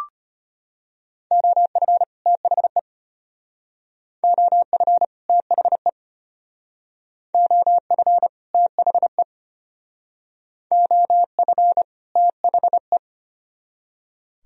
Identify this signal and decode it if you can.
Morse code, heard as a keyed tone, about 32 wpm, 700 Hz. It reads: OF THE OF THE OF THE OF THE